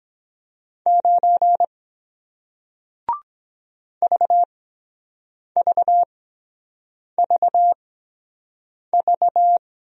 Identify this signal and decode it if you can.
Morse code, heard as a keyed tone, about 23 wpm, spaced as for 7 wpm, 700 Hz. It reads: 9 VVVV